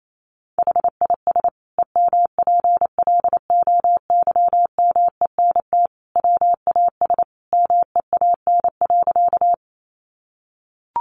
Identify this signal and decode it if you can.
Morse code, heard as a keyed tone, about 28 wpm, 700 Hz. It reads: HIS EMPLOYMENT WAS MEAN.
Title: Sentences from Top 1500 Words 28wpm